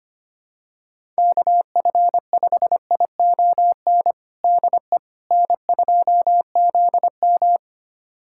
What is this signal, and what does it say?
Morse code, heard as a keyed tone, about 25 wpm, 700 Hz. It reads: KF5ION DE N2ZM
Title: USA Callsign DE Callsign 25wpm